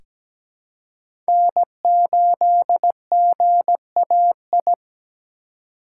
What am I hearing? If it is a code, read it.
Morse code, heard as a keyed tone, about 17 wpm, 700 Hz. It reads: N8GAI
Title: USA Call Signs - 5 Characters - 3x Character Spacing - 17wpm